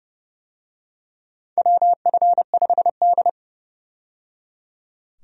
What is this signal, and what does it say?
Morse code, heard as a keyed tone, about 31 wpm, 700 Hz. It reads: WF5D
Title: USA Call Signs - 4 Characters - 30wpm